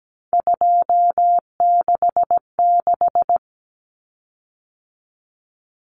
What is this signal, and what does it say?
Morse code, heard as a keyed tone, about 17 wpm, 700 Hz. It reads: 266
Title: Common Cut Numbers - 3-Digit 17wpm